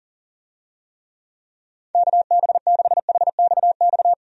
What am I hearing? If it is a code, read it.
Morse code, heard as a keyed tone, about 40 wpm, 700 Hz. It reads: KB6HXX
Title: USA Call Signs - 6 Characters - 40wpm